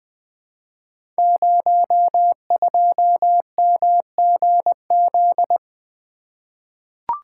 Morse code, heard as a keyed tone, about 20 wpm, 700 Hz. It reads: 02MGZ
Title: Code Groups - 5 Letter:Numbers - 20wpm